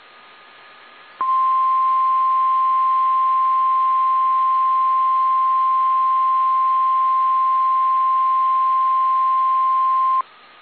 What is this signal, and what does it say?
Emergency Alert System (SAME) tones (the 1050 Hz attention tone). It is heard from 1.2 to 10.2 s.
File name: eas / KSDZKDJL / KSDZKDJL